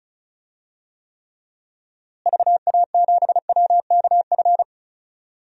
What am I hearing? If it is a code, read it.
Morse code, heard as a keyed tone, about 36 wpm, 700 Hz. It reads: VA7WKF
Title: DXCC-100 Call Signs - 6 Characters - Repeated 3x - 35wpm